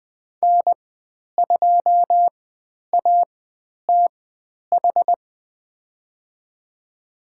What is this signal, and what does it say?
Morse code, heard as a keyed tone, about 20 wpm, spaced as for 10 wpm, 700 Hz. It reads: N2ATH
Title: 109 Lesson 18g - Letter m - Call Signs 20wpm